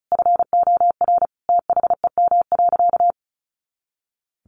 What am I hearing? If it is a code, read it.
Morse code, heard as a keyed tone, about 34 wpm, 700 Hz. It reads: FOR THEM.
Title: Sentences from Top 300 Words 35wpm